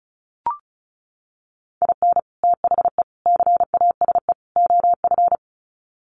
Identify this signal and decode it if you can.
Morse code, heard as a keyed tone, about 34 wpm, 700 Hz. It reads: IN THE CASE OF
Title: Sets of 4 Words 35wpm